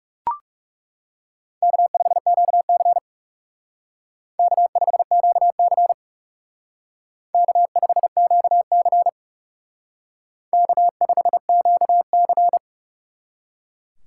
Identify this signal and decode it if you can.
Morse code, heard as a keyed tone, about 37 wpm, 700 Hz. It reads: K5QC K5QC K5QC K5QC
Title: Speed Racing - USA Call Signs 30wpm